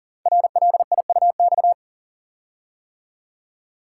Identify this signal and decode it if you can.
Morse code, heard as a keyed tone, about 42 wpm, 700 Hz. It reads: RLIUX
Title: Code Groups - 5 Letter:Numbers - 40wpm